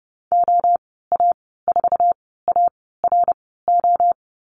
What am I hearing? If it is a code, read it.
Morse code, heard as a keyed tone, about 30 wpm, spaced as for 17 wpm, 700 Hz. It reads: OA4ARO
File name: Lesson 14g - Number 2 - Call Signs 30wpm